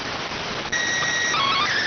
SSTV (slow-scan television), PD 290.